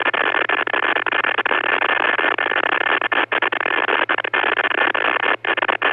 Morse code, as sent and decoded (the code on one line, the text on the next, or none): none